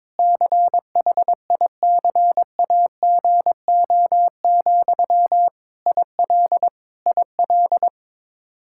Morse code, heard as a keyed tone, about 22 wpm, 700 Hz. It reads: CHICAGO, IL IL
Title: K1USN SST QSO Practice 22wpm